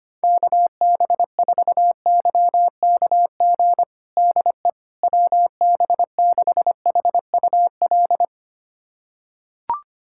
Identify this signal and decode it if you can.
Morse code, heard as a keyed tone, about 25 wpm, 700 Hz. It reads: KB4YKG DE WB6HUL
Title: USA Callsign DE Callsign 25wpm